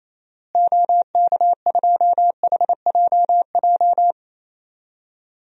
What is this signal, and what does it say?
Morse code, heard as a keyed tone, about 28 wpm, 700 Hz. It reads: OK2HJJ
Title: DXCC-100 Call Signs - 6 Characters - Repeated 2x - 28wpm